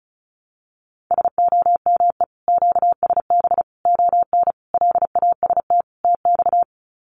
Morse code, heard as a keyed tone, about 34 wpm, 700 Hz. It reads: SOME QSB ON LAST TX
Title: Common QSO Phrases 35wpm